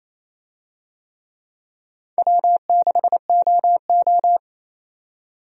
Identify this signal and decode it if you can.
Morse code, heard as a keyed tone, about 27 wpm, 700 Hz. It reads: W6OO
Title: USA Call Signs - 4 Characters - 28wpm